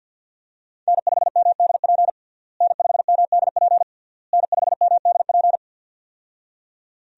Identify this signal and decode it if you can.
Morse code, heard as a keyed tone, about 51 wpm, 700 Hz. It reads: N5MDP N5MDP N5MDP
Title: USA Call Signs - 5 Characters - Repeated 3x - 50wpm